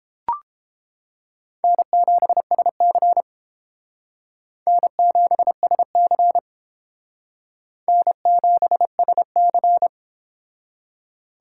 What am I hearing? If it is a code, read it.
Morse code, heard as a keyed tone, about 28 wpm, 700 Hz. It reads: N7SC N7SC N7SC